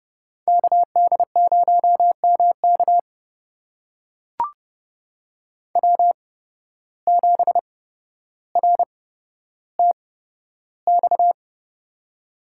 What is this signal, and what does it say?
Morse code, heard as a keyed tone, about 30 wpm, spaced as for 13 wpm, 700 Hz. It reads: KD0MK W7RTX
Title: USA Call Signs - 5 Characters - 8x Character Spacing - 30wpm